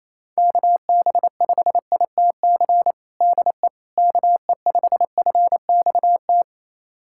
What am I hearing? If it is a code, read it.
Morse code, heard as a keyed tone, about 28 wpm, 700 Hz. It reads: KB5ITC DE KE5FXT
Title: USA Callsign DE Callsign - 6 Characters - 28wpm